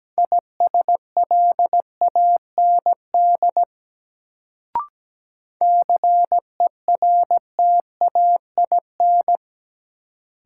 Morse code, heard as a keyed tone, about 17 wpm, 700 Hz. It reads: ISLAND CERTAIN
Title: Sentences from Top 500 Words - New Word Drill - 17wpm